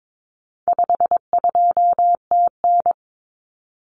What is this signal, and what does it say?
Morse code, heard as a keyed tone, about 22 wpm, 700 Hz. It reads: 52TN
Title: Common Cut Numbers - 4-Digit 22wpm